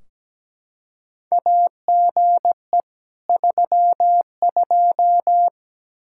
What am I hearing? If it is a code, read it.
Morse code, heard as a keyed tone, about 17 wpm, 700 Hz. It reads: AGE 32
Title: K1USN SST QSO Practice 17wpm